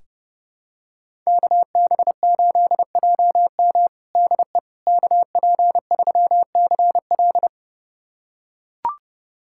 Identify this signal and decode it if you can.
Morse code, heard as a keyed tone, about 32 wpm, 700 Hz. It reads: KB8JM DE KP3CL
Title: USA Callsign DE Callsign - 5 Characters - 30wpm